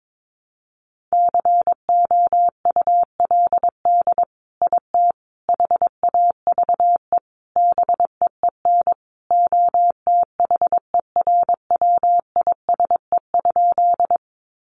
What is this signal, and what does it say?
Morse code, heard as a keyed tone, about 22 wpm, 700 Hz. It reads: COULD IT HAVE BEEN OTHERWISE?
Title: Sentences from Top 1200 Words 22wpm